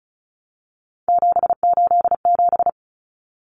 Morse code, heard as a keyed tone, about 34 wpm, 700 Hz. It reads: 787
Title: Common Cut Numbers - 3-Digit 35wpm